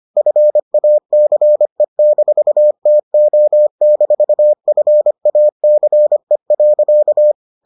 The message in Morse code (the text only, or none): FACE-TO-FACE.